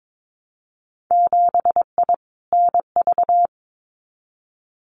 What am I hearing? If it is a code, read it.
Morse code, heard as a keyed tone, about 22 wpm, 700 Hz. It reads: 7I NV
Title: Winter Field Day Exchanges Encore 22wpm